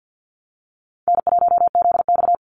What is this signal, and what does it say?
Morse code, heard as a keyed tone, about 50 wpm, 700 Hz. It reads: N1ZX
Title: Easy Call Signs Repeated 2x - 50wpm